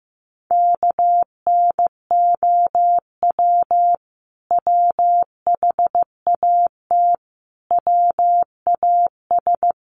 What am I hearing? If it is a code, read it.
Morse code, heard as a keyed tone, about 15 wpm, 700 Hz. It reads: KNOW WHAT WAS
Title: Sets of 5 Words Encore2 15wpm